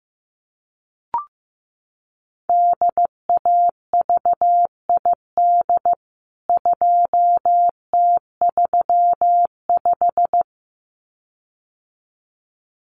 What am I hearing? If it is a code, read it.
Morse code, heard as a keyed tone, about 15 wpm, 700 Hz. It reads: DAVID 2T35
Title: CWT Exchange 15wpm